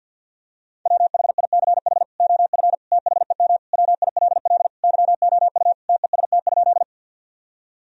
Morse code, heard as a keyed tone, about 51 wpm, 700 Hz. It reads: WHICH OF THEM WILL YOU TEST?
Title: Sentences from Top 1000 Words - Review Part 3 - 50wpm